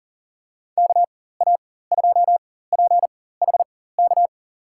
Morse code, heard as a keyed tone, about 40 wpm, spaced as for 19 wpm, 700 Hz. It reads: KA2PHK
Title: USA Call Signs - 4x Character Spacing - 40wpm